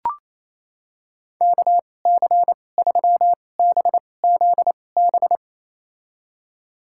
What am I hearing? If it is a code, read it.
Morse code, heard as a keyed tone, about 28 wpm, spaced as for 20 wpm, 700 Hz. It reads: KC3BZB
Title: USA Call Signs - 6 Characters - 2x Character Spacing - 28wpm